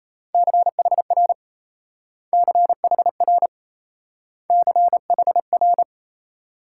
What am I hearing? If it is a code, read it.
Morse code, heard as a keyed tone, about 36 wpm, 700 Hz. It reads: CHR CHR CHR